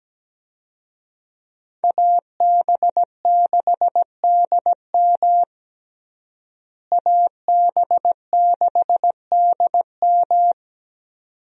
Morse code, heard as a keyed tone, about 17 wpm, 700 Hz. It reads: AB6DM AB6DM
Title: USA Call Signs - 5 Characters - Repeated 2x - 17wpm